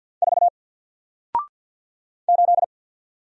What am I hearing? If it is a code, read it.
Morse code, heard as a keyed tone, about 48 wpm, 700 Hz. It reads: 4 8
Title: Single Character (Letter:Number:Punctuation:Prosign) 50wpm